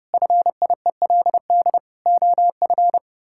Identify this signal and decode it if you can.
Morse code, heard as a keyed tone, about 32 wpm, 700 Hz. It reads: FIELD OF
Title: Pangram Challenge v2 - 30wpm